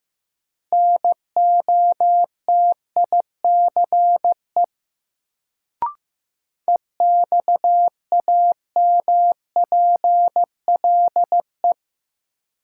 Morse code, heard as a keyed tone, about 15 wpm, 700 Hz. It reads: NOTICE EXAMPLE
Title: Sentences from Top 400 Words - New Word Drill - 15wpm